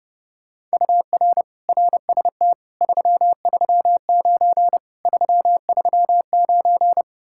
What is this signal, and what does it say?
Morse code, heard as a keyed tone, about 35 wpm, 700 Hz. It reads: UR RST 339 339
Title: Speed-Racing Common QSO Phrases - 30wpm